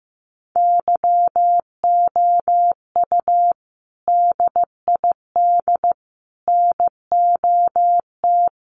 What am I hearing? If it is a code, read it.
Morse code, heard as a keyed tone, about 15 wpm, 700 Hz. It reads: YOU DID NOT
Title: Sets of 5 Words Encore 15wpm